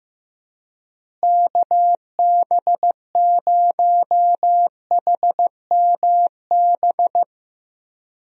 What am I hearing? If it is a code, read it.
Morse code, heard as a keyed tone, about 15 wpm, 700 Hz. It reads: KB0HMB
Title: USA Call Signs - 6 Characters - Repeated 2x - 15wpm